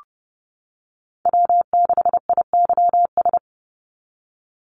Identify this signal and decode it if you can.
Morse code, heard as a keyed tone, about 31 wpm, 700 Hz. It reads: W6IYS
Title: NA SOTA Chaser Callsigns 30wpm